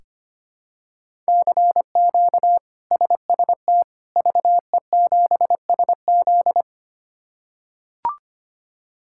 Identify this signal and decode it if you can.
Morse code, heard as a keyed tone, about 25 wpm, 700 Hz. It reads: CQ SST VE7SZ
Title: K1USN SST QSO Practice - 25wpm